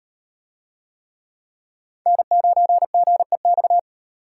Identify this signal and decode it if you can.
Morse code, heard as a keyed tone, about 38 wpm, 700 Hz. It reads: N9GEX